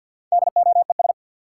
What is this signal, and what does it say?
Morse code, heard as a keyed tone, about 53 wpm, 700 Hz. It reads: DOES
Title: Four Letter Words - 50wpm